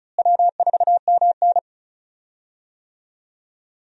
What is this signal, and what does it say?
Morse code, heard as a keyed tone, about 34 wpm, 700 Hz. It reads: W4MN
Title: USA Call Signs - 4 Characters - Repeated 3x - 35wpm